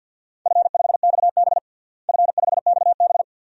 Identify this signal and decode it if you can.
Morse code, heard as a keyed tone, about 52 wpm, 700 Hz. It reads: F5XB F5XB
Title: DXCC-100 Call Signs - 4 Characters - Repeated 3x - 50wpm